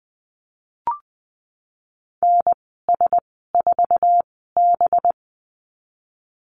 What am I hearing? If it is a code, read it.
Morse code, heard as a keyed tone, about 20 wpm, spaced as for 15 wpm, 700 Hz. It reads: NS4B
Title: 193 Lesson 31g - Letter v - Call Signs 20wpm